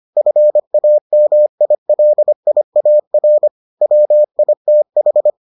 Morse code, heard as a keyed tone, about 25 wpm, 600 Hz. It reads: FAMILIAR WITH